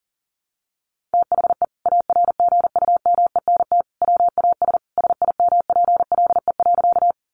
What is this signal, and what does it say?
Morse code, heard as a keyed tone, about 40 wpm, 700 Hz. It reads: THE ARGUMENT WAS SIMPLE.
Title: Sentences from Top 1400 Words 40wpm